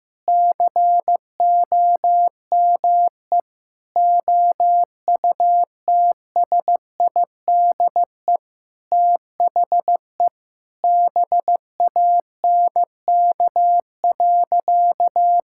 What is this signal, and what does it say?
Morse code, heard as a keyed tone, about 15 wpm, 700 Hz. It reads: COME OUTSIDE THE BANK.
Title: Sentences from Top 1000 Words - Review Part 6 - 15wpm